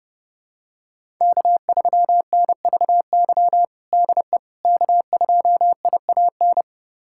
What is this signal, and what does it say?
Morse code, heard as a keyed tone, about 30 wpm, 700 Hz. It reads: K3NVY DE K2IAN
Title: USA Callsign DE Callsign - 5 Characters - 30wpm